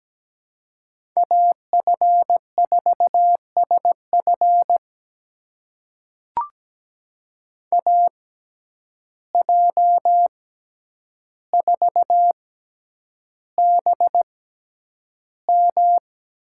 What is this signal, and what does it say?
Morse code, heard as a keyed tone, about 17 wpm, spaced as for 9 wpm, 700 Hz. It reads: AF4SF AJ4BM
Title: USA Call Signs - 5 Characters - 6x Character Spacing - 17wpm